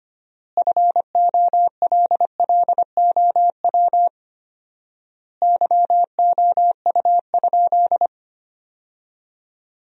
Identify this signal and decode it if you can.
Morse code, heard as a keyed tone, about 25 wpm, 700 Hz. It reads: FOLLOW YOU?
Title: 4x Word-Spacing - Sentences from Top 200 Words - 25wpm